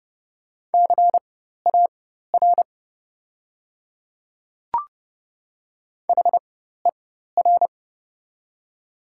Morse code, heard as a keyed tone, about 30 wpm, spaced as for 14 wpm, 700 Hz. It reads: CAR HER
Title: 4x Character Spacing - Top 500 Words - 30wpm